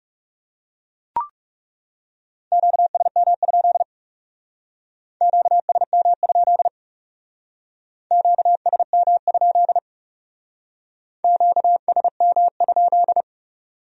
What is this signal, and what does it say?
Morse code, heard as a keyed tone, about 37 wpm, 700 Hz. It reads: QSM? QSM? QSM? QSM?
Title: Speed-Racing Q-Codes - 30wpm